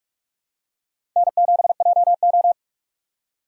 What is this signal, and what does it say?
Morse code, heard as a keyed tone, about 46 wpm, 700 Hz. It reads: N7JO
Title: USA Call Signs - 4 Characters - 2x Character Spacing - 45wpm